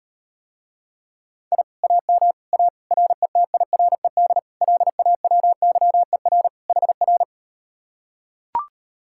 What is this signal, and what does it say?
Morse code, heard as a keyed tone, about 38 wpm, 700 Hz. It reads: I AM A RETIRED LAWYER HR